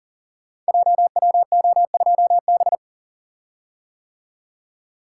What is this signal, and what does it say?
Morse code, heard as a keyed tone, about 43 wpm, 700 Hz. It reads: JWO2B
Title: Code Groups - 5 Letter:Numbers - 40wpm